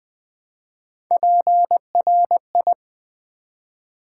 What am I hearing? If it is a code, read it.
Morse code, heard as a keyed tone, about 20 wpm, 700 Hz. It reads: PRI
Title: Build A Word - 1 Letter At A Time - 5-Letter Words - 20wpm